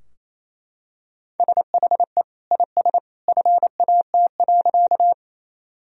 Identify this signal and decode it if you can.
Morse code, heard as a keyed tone, about 28 wpm, 700 Hz. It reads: SHE IS FAT.
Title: Sentences from Top 1000 Words - Review Part 4 - 28wpm